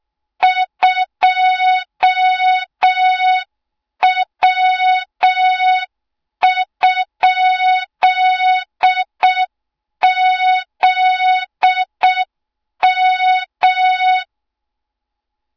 Morse code, heard as a keyed tone, about 6 wpm, 3700 Hz. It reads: 2W?ZM
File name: Morze-HA2MN / hang / 20 GYAKORLAT